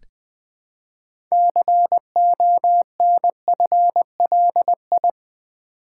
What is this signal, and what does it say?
Morse code, heard as a keyed tone, about 20 wpm, 700 Hz. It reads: CONFLI
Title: Build A Word - 3 Letters At A Time - 8-Letter Words - 20wpm